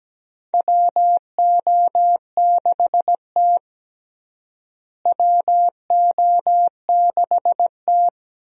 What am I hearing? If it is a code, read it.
Morse code, heard as a keyed tone, about 17 wpm, 700 Hz. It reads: WO6T WO6T
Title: USA Call Signs - 4 Characters - Repeated 3x - 17wpm